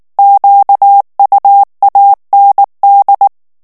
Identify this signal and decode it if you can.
Morse code, heard as a keyed tone, about 19 wpm, 800 Hz. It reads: QUAND
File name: morse mp3 / FRANCAIS TEXTE CLAIR 1 VITESSE 18 MOTS MINUTES